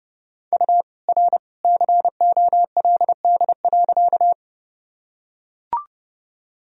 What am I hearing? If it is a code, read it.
Morse code, heard as a keyed tone, about 30 wpm, 700 Hz. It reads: U R COLD.
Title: Sentences from Top 400 Words and Abbreviations 30wpm